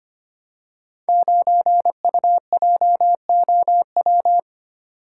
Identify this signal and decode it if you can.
Morse code, heard as a keyed tone, about 25 wpm, 690 Hz. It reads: 9UJOW
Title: Code Groups - 5 Letter:Numbers - 25wpm